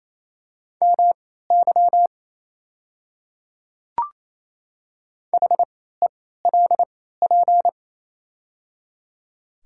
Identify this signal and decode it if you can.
Morse code, heard as a keyed tone, about 29 wpm, spaced as for 16 wpm, 690 Hz. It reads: MY HELP
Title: 3x Character Spacing - Top 300 Words - 28wpm